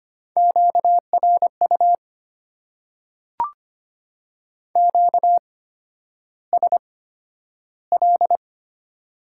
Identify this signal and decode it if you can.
Morse code, heard as a keyed tone, about 25 wpm, spaced as for 11 wpm, 690 Hz. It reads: QRU QSL
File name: Top 90 QSO Elements - 8x Character Spacing - 25wpm